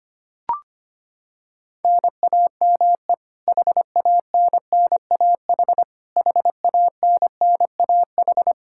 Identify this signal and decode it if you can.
Morse code, heard as a keyed tone, about 25 wpm, 700 Hz. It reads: NAME HANNAH HANNAH
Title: Practice QSO - Exchange 1 - Beginning - 25wpm